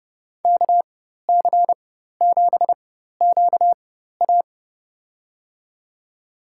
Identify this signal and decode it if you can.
Morse code, heard as a keyed tone, about 32 wpm, spaced as for 14 wpm, 700 Hz. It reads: KC7QA
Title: USA Call Signs - 5 Characters - 4x Character Spacing - 30wpm